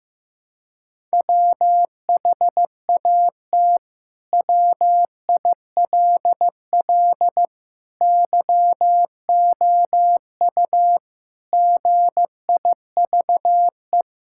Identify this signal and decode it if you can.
Morse code, heard as a keyed tone, about 15 wpm, 700 Hz. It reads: WHAT WILL YOU GIVE
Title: Sentences from Top 1000 Words - Review Part 6 - 15wpm